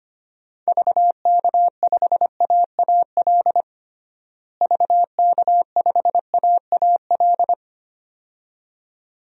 Morse code, heard as a keyed tone, about 25 wpm, 700 Hz. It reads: VK5AAL VK5AAL
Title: DXCC-100 Call Signs - 6 Characters - Repeated 2x - 25wpm